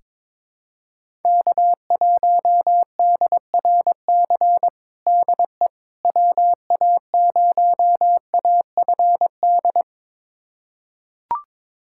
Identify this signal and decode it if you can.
Morse code, heard as a keyed tone, about 22 wpm, 700 Hz. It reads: K1DRC DE WA0AFD
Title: USA Callsign DE Callsign 22wpm